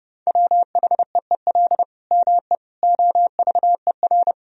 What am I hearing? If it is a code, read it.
Morse code, heard as a keyed tone, about 30 wpm, 700 Hz. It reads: WHEEL ME OVER
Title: Sentences from Top 1000 Words - Review Part 3 - 30wpm